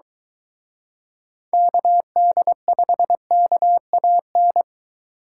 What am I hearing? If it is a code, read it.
Morse code, heard as a keyed tone, about 23 wpm, 700 Hz. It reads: KD5KAN